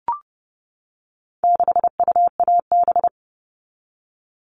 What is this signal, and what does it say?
Morse code, heard as a keyed tone, about 30 wpm, 700 Hz. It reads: 6UAB
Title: All Cut Numbers - 4-Digit 30wpm